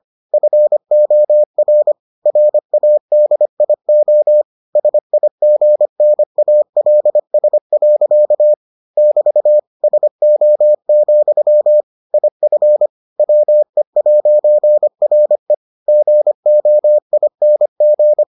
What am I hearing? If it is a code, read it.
Morse code, heard as a keyed tone, about 25 wpm, 600 Hz. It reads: FOR RADIO SIGNALS. <BT> SO, IF WE'RE GOING